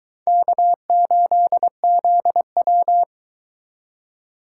Morse code, heard as a keyed tone, about 22 wpm, 700 Hz. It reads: K8ZW